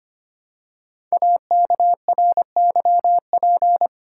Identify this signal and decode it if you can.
Morse code, heard as a keyed tone, about 25 wpm, 700 Hz. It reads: AKRYP
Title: Code Groups - 5 Letters - 25wpm